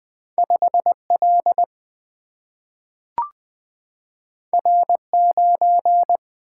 Morse code, heard as a keyed tone, about 20 wpm, 700 Hz. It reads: 5 L R 9
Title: Single-Letter Single-Number 20wpm